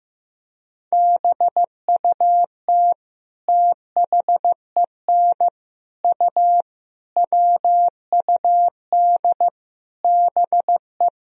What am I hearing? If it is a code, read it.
Morse code, heard as a keyed tone, about 15 wpm, 700 Hz. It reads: BUT THEN U WUD BE
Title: Sentences from Top 300 Words and Abbreviations 15wpm